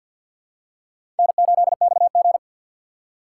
Morse code, heard as a keyed tone, about 51 wpm, 700 Hz. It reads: N8XG
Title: USA Call Signs - 6x Character Spacing - 50wpm